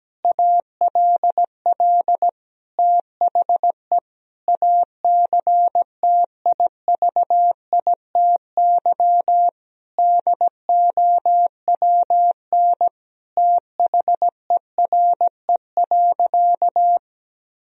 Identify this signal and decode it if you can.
Morse code, heard as a keyed tone, about 17 wpm, 700 Hz. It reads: ALL THE ACTIVITY DOWN THERE.
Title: Sentences from Top 1000 Words - Review Part 4 - 17wpm